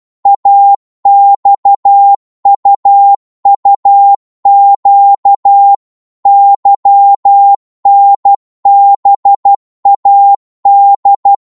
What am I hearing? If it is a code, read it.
Morse code, heard as a keyed tone, about 12 wpm, 800 Hz. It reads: AXUUQ YNBAD
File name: cwmp3 / ltr-12wpm